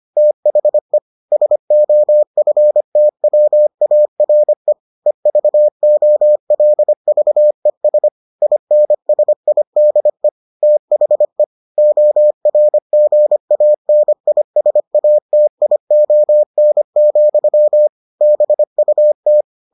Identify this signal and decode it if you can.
Morse code, heard as a keyed tone, about 25 wpm, 600 Hz. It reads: THE SOFTWARE EVOLVES INSIDE THE ORGANISATION, BUT